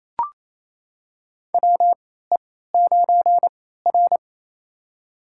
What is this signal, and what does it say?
Morse code, heard as a keyed tone, about 28 wpm, spaced as for 16 wpm, 700 Hz. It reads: WE9R
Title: USA Call Signs - 4 Characters - 3x Character Spacing - 28wpm